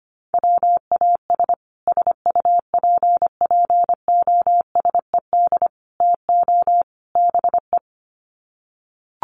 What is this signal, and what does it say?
Morse code, heard as a keyed tone, about 25 wpm, 700 Hz. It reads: WAS SUPPOSED TO BE
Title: Sets of 4 Words 25wpm